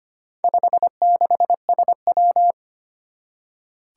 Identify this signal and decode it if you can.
Morse code, heard as a keyed tone, about 25 wpm, 700 Hz. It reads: 56SW
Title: Four Challenging Letter-Number Combinations 25wpm